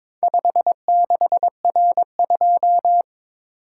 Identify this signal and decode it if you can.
Morse code, heard as a keyed tone, about 22 wpm, 700 Hz. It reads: 56R2
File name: practice / 22wpm / Four Letter-Number Combinations - 22wpm